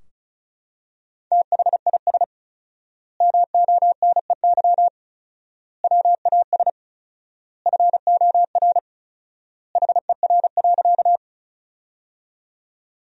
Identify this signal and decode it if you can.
Morse code, heard as a keyed tone, about 36 wpm, 700 Hz. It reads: THIS MONEY WAS FOR HER.
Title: 4x Word-Spacing - Sentences from Top 400 Words - 35wpm